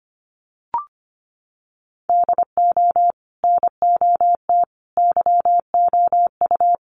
Morse code, heard as a keyed tone, about 25 wpm, 700 Hz. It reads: DO NOT YOU
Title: Sets of 5 Words Encore2 25wpm